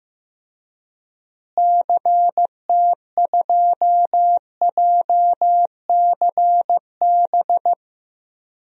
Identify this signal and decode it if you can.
Morse code, heard as a keyed tone, about 15 wpm, 700 Hz. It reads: CT2JCB
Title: DXCC-100 Call Signs - 6 Characters - Repeated 2x - 15wpm